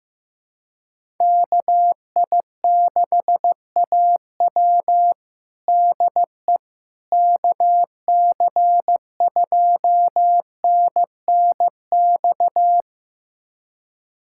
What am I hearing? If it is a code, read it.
Morse code, heard as a keyed tone, about 15 wpm, 700 Hz. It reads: KI6AW DE KC2NNX
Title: USA Callsign DE Callsign 15wpm